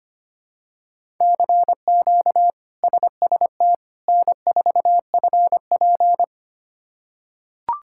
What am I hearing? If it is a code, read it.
Morse code, heard as a keyed tone, about 25 wpm, 700 Hz. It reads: CQ SST N4FP
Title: K1USN SST QSO Practice - 25wpm